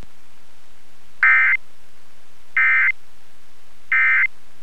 Emergency Alert System (SAME) tones, heard as harsh buzzing bursts. It is heard from 1.2 to 4.3 s.